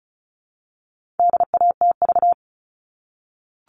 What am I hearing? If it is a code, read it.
Morse code, heard as a keyed tone, about 34 wpm, 700 Hz. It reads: DATV
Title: All Cut Numbers - 4-Digit 35wpm